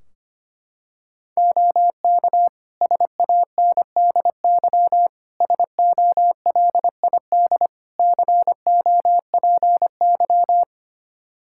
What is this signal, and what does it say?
Morse code, heard as a keyed tone, about 25 wpm, 700 Hz. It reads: OK SANDY SOLID COPY
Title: Practice QSO - Exchange 3 - More Info - 25wpm